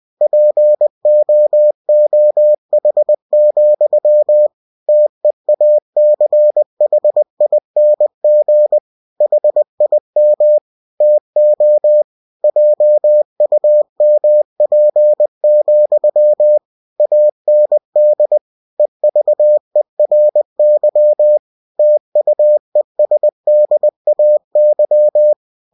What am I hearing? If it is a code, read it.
Morse code, heard as a keyed tone, about 20 wpm, 600 Hz. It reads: POOH, TEACHING HIM TO JUMP, AND EVERY TUESDAY